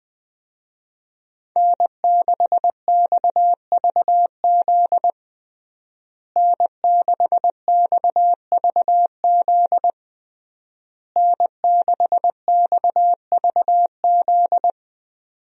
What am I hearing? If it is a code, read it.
Morse code, heard as a keyed tone, about 20 wpm, 700 Hz. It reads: N6XVZ N6XVZ N6XVZ
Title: USA Call Signs - 5 Characters - Repeated 3x - 20wpm